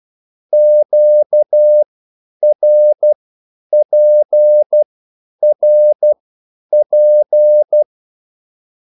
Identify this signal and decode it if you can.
Morse code, heard as a keyed tone, about 12 wpm, spaced as for 9 wpm, 600 Hz. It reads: QRPRP
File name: QSL / CW lesson